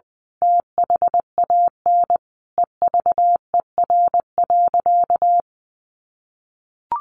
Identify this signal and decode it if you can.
Morse code, heard as a keyed tone, about 20 wpm, 700 Hz. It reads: THAN EVER.
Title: Sentences from Top 1200 Words 20wpm